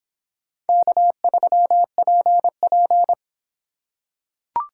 Morse code, heard as a keyed tone, about 26 wpm, 700 Hz. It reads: K3PP